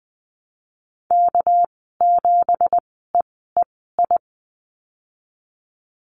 Morse code, heard as a keyed tone, about 20 wpm, spaced as for 15 wpm, 700 Hz. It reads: K7EEI
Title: 200 Lesson 32g - Letter k - Call Signs 20wpm